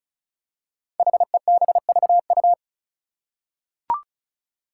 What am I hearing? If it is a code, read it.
Morse code, heard as a keyed tone, about 36 wpm, 700 Hz. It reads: HEBVU
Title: Five Challenging Letter-Number Combinations 35wpm